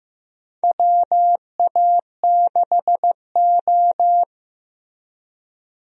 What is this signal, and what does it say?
Morse code, heard as a keyed tone, about 15 wpm, 700 Hz. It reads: WA6O
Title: USA Call Signs - 4 Characters - Repeated 2x - 15wpm